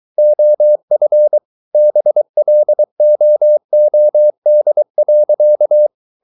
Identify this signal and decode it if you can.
Morse code, heard as a keyed tone, about 23 wpm, 600 Hz. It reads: OF BLOOD.